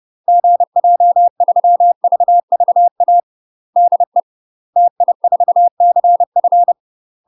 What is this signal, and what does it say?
Morse code, heard as a keyed tone, about 30 wpm, 700 Hz. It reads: GJ3VVA DE TI4CF
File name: UK MASTER QSOs TRAINING / 30WPM / UK MASTER QSOs 0023